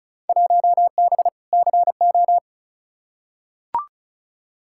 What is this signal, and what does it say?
Morse code, heard as a keyed tone, about 36 wpm, 700 Hz. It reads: 1B CO
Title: Field Day Exchanges - 35wpm